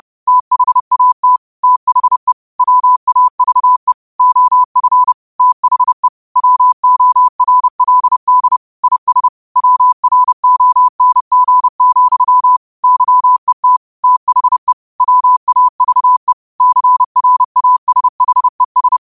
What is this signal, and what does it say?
Morse code, heard as a keyed tone, about 30 wpm, 1000 Hz. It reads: THAT THE WAVE OF THE WORLD IS WRONG, YET THE WAVE CRASHES
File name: qotdfiles / 30wpm